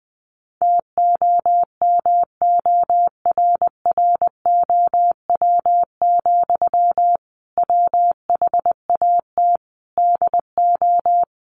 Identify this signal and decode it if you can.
Morse code, heard as a keyed tone, about 20 wpm, 700 Hz. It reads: TOMORROW, WHAT DO
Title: Sentences from Top 1500 Words 20wpm